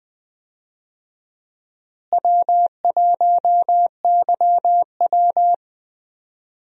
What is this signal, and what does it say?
Morse code, heard as a keyed tone, about 21 wpm, 700 Hz. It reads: W1YW
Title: USA Call Signs - 4 Characters - Repeated 2x - 20wpm